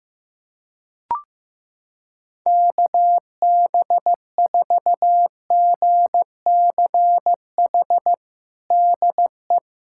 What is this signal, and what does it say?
Morse code, heard as a keyed tone, about 15 wpm, 700 Hz. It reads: KB4GCH DE
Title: USA Callsign DE Callsign 15wpm